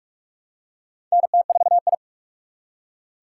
Morse code, heard as a keyed tone, about 42 wpm, 700 Hz. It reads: NT4I
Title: USA Call Signs - 6x Character Spacing - 45wpm